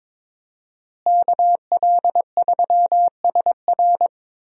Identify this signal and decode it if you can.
Morse code, heard as a keyed tone, about 22 wpm, 700 Hz. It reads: KL3SR
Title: USA Call Signs - 5 Characters - Repeated 3x - 22wpm